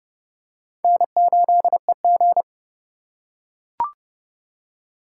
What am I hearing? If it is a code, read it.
Morse code, heard as a keyed tone, about 28 wpm, 700 Hz. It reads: N8EG